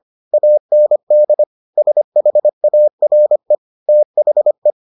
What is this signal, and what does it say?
Morse code, heard as a keyed tone, about 25 wpm, 600 Hz. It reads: AND SHARE THE